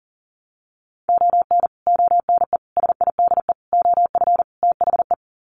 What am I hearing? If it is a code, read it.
Morse code, heard as a keyed tone, about 40 wpm, 700 Hz. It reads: ON ONE SIDE OF THE
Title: Sets of 5 Words 40wpm